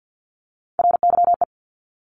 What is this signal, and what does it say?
Morse code, heard as a keyed tone, about 50 wpm, 700 Hz. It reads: RYE
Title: Multi-letter Phonemes 50wpm